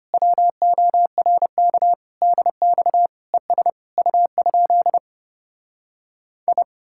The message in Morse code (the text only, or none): WORK DX ES U? I